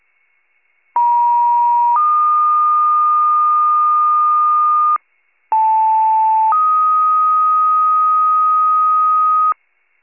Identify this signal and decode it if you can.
Two-tone page, heard as a one-second tone, then a three-second tone, about 950 then 1220 Hz, about 880 then 1250 Hz. It is heard from 1.0 to 9.5 s.
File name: wcalert